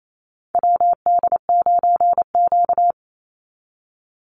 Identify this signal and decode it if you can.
Morse code, heard as a keyed tone, about 28 wpm, 700 Hz. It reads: WD9Q
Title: Call Signs Repeated 3x - 28wpm